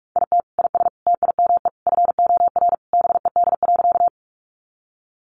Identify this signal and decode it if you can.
Morse code, heard as a keyed tone, about 44 wpm, 700 Hz. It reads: IT IS TIME FOR BED.
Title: Sentences from Top 500 Words 45wpm